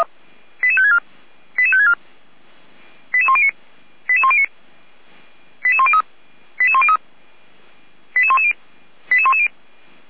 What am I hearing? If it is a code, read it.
Five-tone selective call (ZVEI), heard as quick runs of five tones, tones of about 70 ms. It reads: 88564 88564 88109 88109 88103 88103 88110 88110
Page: /455/SDR/ZVEI/